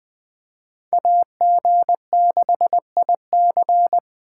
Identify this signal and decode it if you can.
Morse code, heard as a keyed tone, about 24 wpm, 700 Hz. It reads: AG6IC
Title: Speed Racing - USA 2x2 Call Signs 20wpm